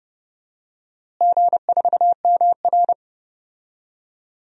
Morse code, heard as a keyed tone, about 30 wpm, 700 Hz. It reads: G4MR
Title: DXCC-100 Call Signs - 4 Characters - Repeated 3x - 30wpm